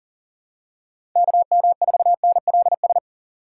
Morse code, heard as a keyed tone, about 40 wpm, 690 Hz. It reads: KM4NLS
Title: USA Call Signs - 6 Characters - 2x Character Spacing - 40wpm